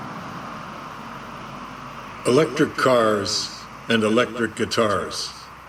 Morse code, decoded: none

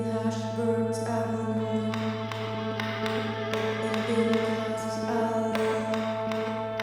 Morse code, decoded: none

